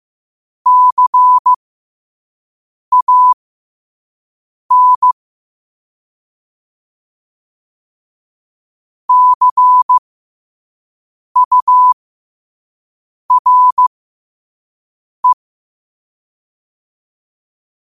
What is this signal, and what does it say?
Morse code, heard as a keyed tone, about 15 wpm, spaced as for 5 wpm, 1000 Hz. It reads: CAN CURE
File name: qotdfiles / 5wpm